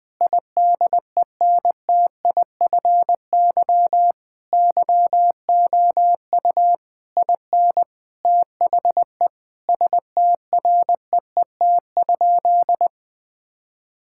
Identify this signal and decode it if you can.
Morse code, heard as a keyed tone, about 21 wpm, 700 Hz. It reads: IDENTIFY YOU IN THE STREET?
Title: Sentences from Top 1000 Words - Review Part 6 - 20wpm